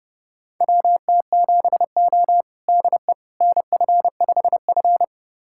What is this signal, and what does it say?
Morse code, heard as a keyed tone, about 30 wpm, 700 Hz. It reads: WT7O DE NF5F
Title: USA Callsign DE Callsign - 4 Characters - 30wpm